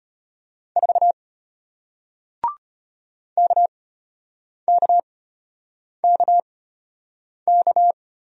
Morse code, heard as a keyed tone, about 34 wpm, spaced as for 8 wpm, 700 Hz. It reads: 4 KKKK